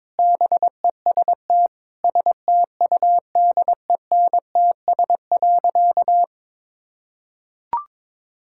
Morse code, heard as a keyed tone, about 22 wpm, 700 Hz. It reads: BEST STUDENTS.
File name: practice / 22wpm / Sentences with 4-Word Phrases - 22wpm